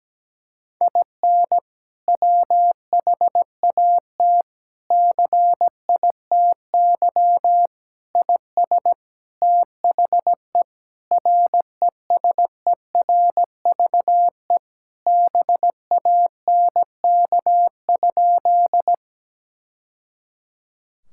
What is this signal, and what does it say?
Morse code, heard as a keyed tone, about 17 wpm, 700 Hz. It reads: IN WHAT CITY IS THE RESERVE BANK?
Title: Sentences from Top 1000 Words - Review Part 6 - 17wpm